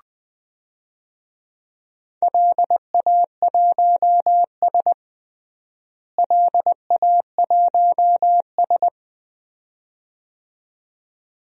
Morse code, heard as a keyed tone, about 21 wpm, 700 Hz. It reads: LA1S LA1S
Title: DXCC-100 Call Signs - 4 Characters - Repeated 2x - 20wpm